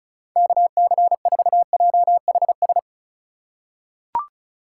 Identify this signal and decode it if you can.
Morse code, heard as a keyed tone, about 36 wpm, 700 Hz. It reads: KC4JHS
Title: USA Call Signs - 6 Characters - 4x Character Spacing - 35wpm